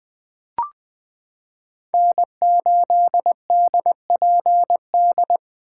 Morse code, heard as a keyed tone, about 21 wpm, 700 Hz. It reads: N8DPD
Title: K1USN SST QSO Practice - 20wpm